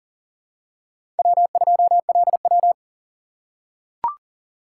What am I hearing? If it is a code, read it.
Morse code, heard as a keyed tone, about 37 wpm, 700 Hz. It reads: W2LW
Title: USA Call Signs - 4 Characters - 2x Character Spacing - 40wpm